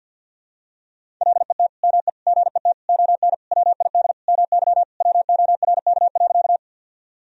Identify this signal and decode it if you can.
Morse code, heard as a keyed tone, about 52 wpm, 700 Hz. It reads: LET ME GET ON WID MY WORK.
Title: Sentences from Top 300 Words and Abbreviations 50wpm